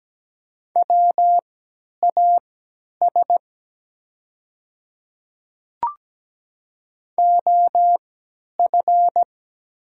Morse code, heard as a keyed tone, about 17 wpm, spaced as for 10 wpm, 700 Hz. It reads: WAS OF
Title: 3x Character Spacing - Top 200 Words - 17wpm